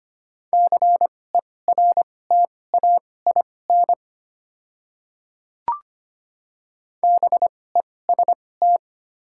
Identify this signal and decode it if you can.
Morse code, heard as a keyed tone, about 25 wpm, spaced as for 18 wpm, 690 Hz. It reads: CERTAIN BEST
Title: 2x Character Spacing - Top 500 Words - 25wpm